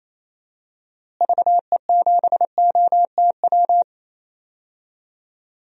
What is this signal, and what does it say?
Morse code, heard as a keyed tone, about 28 wpm, 700 Hz. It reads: VE7OTW
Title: DXCC-100 Call Signs - 6 Characters - Repeated 3x - 28wpm